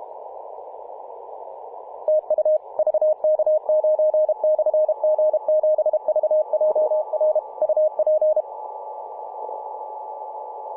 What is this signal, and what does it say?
Morse code, heard as a keyed tone, about 32 wpm, 620 Hz, with radio noise. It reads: TU VK9/G7VJR UP
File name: vk9-g7vjr-24-903mhz-cw